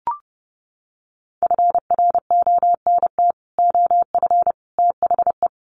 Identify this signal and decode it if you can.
Morse code, heard as a keyed tone, about 30 wpm, 700 Hz. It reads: FRONT OF THE
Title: Sets of 3 Words 30wpm